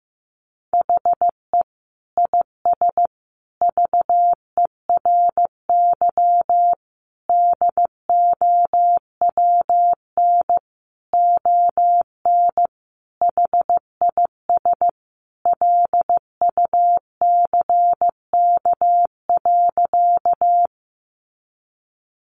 Morse code, heard as a keyed tone, about 15 wpm, 700 Hz. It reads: HE IS VERY DOWN ON HIS LUCK.
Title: Sentences from Top 1500 Words 15wpm